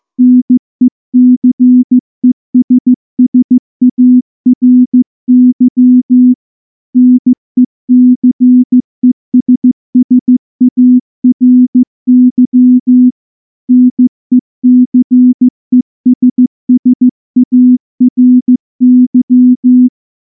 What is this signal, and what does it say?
Morse code, heard as a keyed tone, about 14 wpm, 250 Hz. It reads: NECESSARY NECESSARY NECESSARY
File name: morse / morse-training-15WPM